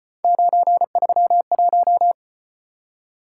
Morse code, heard as a keyed tone, about 35 wpm, 700 Hz. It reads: 931